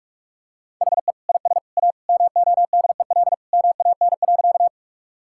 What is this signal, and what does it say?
Morse code, heard as a keyed tone, about 46 wpm, 700 Hz. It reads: HE IS A MODEL MAN.
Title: Sentences from Top 1000 Words - Review Part 5 - 45wpm